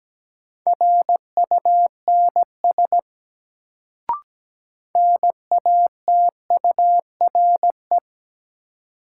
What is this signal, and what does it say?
Morse code, heard as a keyed tone, about 17 wpm, 700 Hz. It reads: RUNS NATURE
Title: Sentences from Top 600 Words - New Word Drill - 17wpm